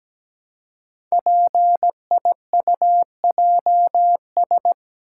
Morse code, heard as a keyed tone, about 17 wpm, 700 Hz. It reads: PIUJS
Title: Code Groups - 5 Letters - 17wpm